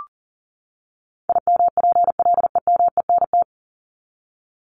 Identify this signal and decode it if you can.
Morse code, heard as a keyed tone, about 40 wpm, 700 Hz. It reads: IMPLEMENT
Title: Top 1501-1750 Words 40wpm